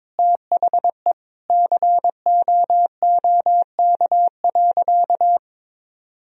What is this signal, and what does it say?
Morse code, heard as a keyed tone, about 22 wpm, 700 Hz. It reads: THE COOK.
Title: Sentences from Top 1000 Words - Review Part 4 - 22wpm